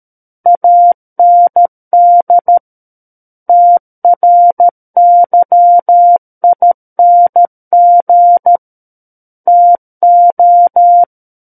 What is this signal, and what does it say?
Morse code, heard as a keyed tone, about 13 wpm, 700 Hz. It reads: AND TRYING TO
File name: Book One / 04-Chapter 4 - 13 WPM